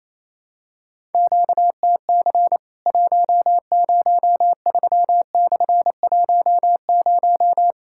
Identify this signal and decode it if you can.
Morse code, heard as a keyed tone, about 28 wpm, 700 Hz. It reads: QTC 103/10
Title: Worked All Europe DX Contest - QTC Exchange - 28wpm